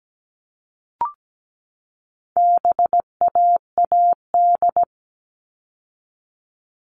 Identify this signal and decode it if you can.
Morse code, heard as a keyed tone, about 17 wpm, 700 Hz. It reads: BAAD
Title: All Cut Numbers - 4-Digit 17wpm